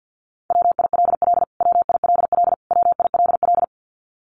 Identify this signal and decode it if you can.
Morse code, heard as a keyed tone, about 50 wpm, 700 Hz. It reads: WILL WILL WILL
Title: Top 200 Words Repeated 50wpm